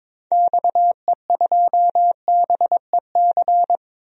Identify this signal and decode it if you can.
Morse code, heard as a keyed tone, about 22 wpm, 700 Hz. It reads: XE2BEC
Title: DXCC-100 Call Signs - 6 Characters - 22wpm